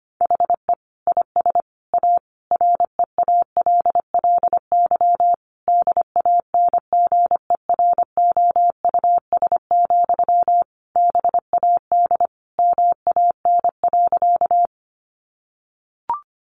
Morse code, heard as a keyed tone, about 25 wpm, 700 Hz. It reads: HE IS A REALLY DANGEROUS, BAD MAN.
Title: Sentences from Top 1200 Words 25wpm